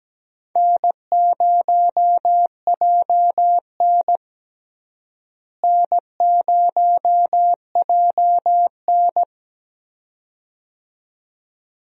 Morse code, heard as a keyed tone, about 17 wpm, 700 Hz. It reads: N0JN N0JN
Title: USA Call Signs - 4 Characters - Repeated 2x - 17wpm